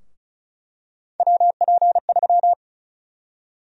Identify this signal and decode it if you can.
Morse code, heard as a keyed tone, about 36 wpm, 700 Hz. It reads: WP3
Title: USA Call Sign Prefixes - Repeated 2x - 35wpm